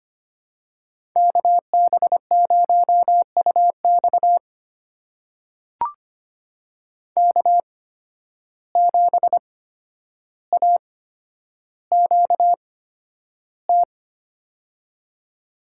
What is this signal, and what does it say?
Morse code, heard as a keyed tone, about 25 wpm, spaced as for 11 wpm, 700 Hz. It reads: KB0UX K7AQT
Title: USA Call Signs - 5 Characters - 8x Character Spacing - 25wpm